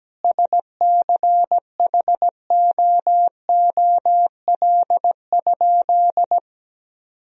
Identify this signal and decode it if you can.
Morse code, heard as a keyed tone, about 17 wpm, 700 Hz. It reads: SCHOOL?
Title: Sentences from Top 1000 Words - Review Part 1 - 17wpm